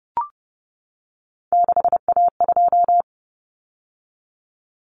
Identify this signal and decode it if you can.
Morse code, heard as a keyed tone, about 30 wpm, 700 Hz. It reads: 6A2
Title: Common Cut Numbers - 3-Digit 30wpm